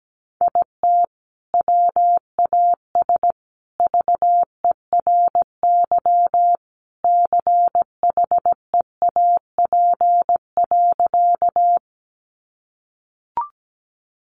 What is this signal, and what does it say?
Morse code, heard as a keyed tone, about 17 wpm, 700 Hz. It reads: IT WAS VERY CHEAP.
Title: Sentences from Top 1200 Words 17wpm